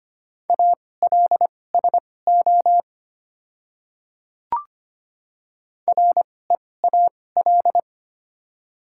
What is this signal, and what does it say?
Morse code, heard as a keyed tone, about 25 wpm, spaced as for 18 wpm, 700 Hz. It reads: ALSO REAL
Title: 2x Character Spacing - Top 300 Words - 25wpm